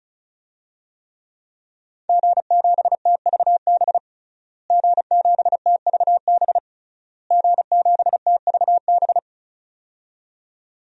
Morse code, heard as a keyed tone, about 34 wpm, 700 Hz. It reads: G7TVB G7TVB G7TVB
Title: DXCC-100 Call Signs - 5 Characters - Repeated 3x - 35wpm